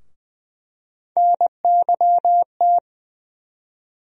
Morse code, heard as a keyed tone, about 20 wpm, 700 Hz. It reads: NYT
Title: USA Call Sign Postfixes - 3 Letters 20wpm